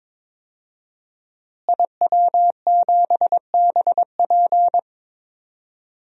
Morse code, heard as a keyed tone, about 22 wpm, 700 Hz. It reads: IW7BP
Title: DXCC-100 Call Signs - 5 Characters - Repeated 3x - 22wpm